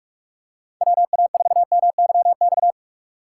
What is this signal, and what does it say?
Morse code, heard as a keyed tone, about 44 wpm, 700 Hz. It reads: W A 4 M Y X
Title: USA Call Signs - 6 Characters - 6x Character Spacing - 45wpm